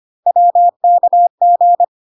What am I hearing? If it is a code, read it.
Morse code, heard as a keyed tone, about 25 wpm, 700 Hz. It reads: WKG
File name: cw / CW common abbreviations alphabetical repeated 3 times 25 25wpm ik1hge